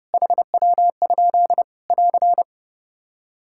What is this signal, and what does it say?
Morse code, heard as a keyed tone, about 29 wpm, 700 Hz. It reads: HW? <AR>
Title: Practice QSO - Exchange 1 - Beginning - 30wpm